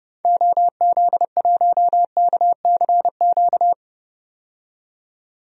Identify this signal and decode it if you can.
Morse code, heard as a keyed tone, about 30 wpm, 700 Hz. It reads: OZ1KCQ
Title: DXCC-100 Call Signs - 6 Characters - 30wpm